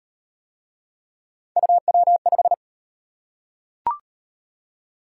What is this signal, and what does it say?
Morse code, heard as a keyed tone, about 34 wpm, 700 Hz. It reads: UW5